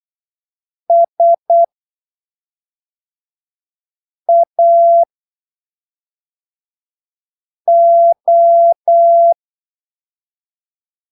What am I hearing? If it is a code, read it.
Morse code, heard as a keyed tone, about 8 wpm, spaced as for 3 wpm, 680 Hz. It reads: SAO
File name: arquivos / texto / texto 26 08ppm